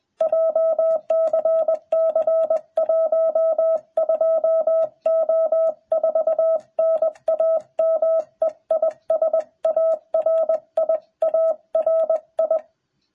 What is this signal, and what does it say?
Morse code, heard as a keyed tone, about 21 wpm, 660 Hz. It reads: JCC 12O4 NAME IS ARI ARI